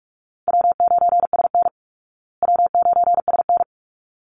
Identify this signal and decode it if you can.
Morse code, heard as a keyed tone, about 44 wpm, 700 Hz. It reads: W9SN W9SN
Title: Easy Call Signs Repeated 2x - 45wpm